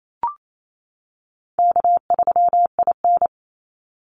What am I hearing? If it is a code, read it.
Morse code, heard as a keyed tone, about 28 wpm, 700 Hz. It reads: K3IN
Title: Easy Call Signs - 28wpm